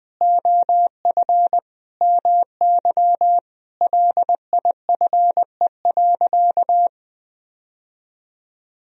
Morse code, heard as a keyed tone, about 20 wpm, 700 Hz. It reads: OF MY LIFE.
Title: Sentences from Top 1000 Words - Review Part 1 - 20wpm